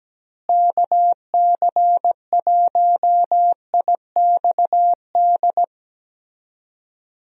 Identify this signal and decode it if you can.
Morse code, heard as a keyed tone, about 17 wpm, 700 Hz. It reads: KC1IXD
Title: USA Call Signs - 6 Characters - Repeated 3x - 17wpm